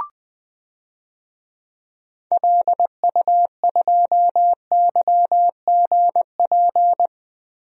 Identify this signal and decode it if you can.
Morse code, heard as a keyed tone, about 20 wpm, 700 Hz. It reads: LU2YGP
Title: DXCC-100 Call Signs - 6 Characters - 20wpm